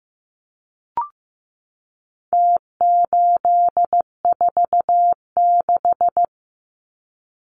Morse code, heard as a keyed tone, about 15 wpm, 700 Hz. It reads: T846
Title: Numbers - Contest Style - 4-Digit 15wpm